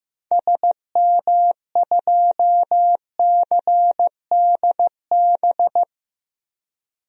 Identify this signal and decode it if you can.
Morse code, heard as a keyed tone, about 16 wpm, 700 Hz. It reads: SM2CDB
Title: DXCC-100 Call Signs - 6 Characters - Repeated 2x - 15wpm